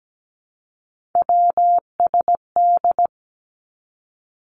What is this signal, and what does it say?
Morse code, heard as a keyed tone, about 17 wpm, 700 Hz. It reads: WSD
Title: Three Challenging Letter-Number Combinations 17wpm